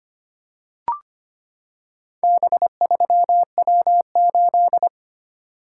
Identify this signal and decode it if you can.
Morse code, heard as a keyed tone, about 25 wpm, 700 Hz. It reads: B3W8
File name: practice / 25wpm / Four Letter-Number Combinations - 25wpm